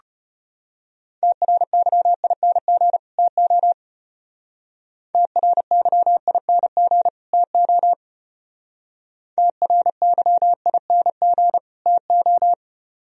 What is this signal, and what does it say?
Morse code, heard as a keyed tone, about 31 wpm, 700 Hz. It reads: TRYING TO TRYING TO TRYING TO
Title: Speed-Racing - Sets of 2 Words 25wpm